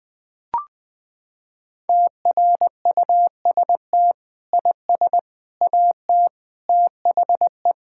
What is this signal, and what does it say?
Morse code, heard as a keyed tone, about 20 wpm, 700 Hz. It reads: TRUST IS AT THE
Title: Sentences with 5-Word Phrases - 20wpm